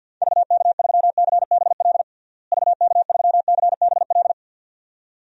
Morse code, heard as a keyed tone, about 51 wpm, 700 Hz. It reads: VK3CBL VK3CBL
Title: DXCC-100 Call Signs - 6 Characters - Repeated 2x - 50wpm